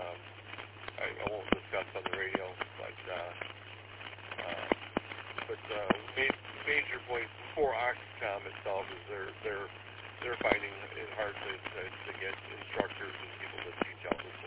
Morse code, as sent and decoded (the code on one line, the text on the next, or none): none